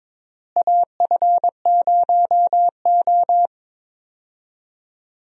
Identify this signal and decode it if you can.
Morse code, heard as a keyed tone, about 22 wpm, 700 Hz. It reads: AF0O
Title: USA Call Signs - 4 Characters - Repeated 3x - 22wpm